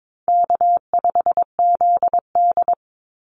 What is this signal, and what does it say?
Morse code, heard as a keyed tone, about 22 wpm, 700 Hz. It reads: K5ZD
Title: Easy Call Signs Repeated 3x - 22wpm